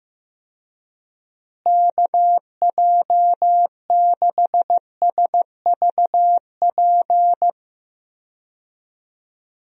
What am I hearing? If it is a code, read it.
Morse code, heard as a keyed tone, about 15 wpm, 700 Hz. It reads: KJ6SVP
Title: USA Call Signs - 6 Characters - Repeated 2x - 15wpm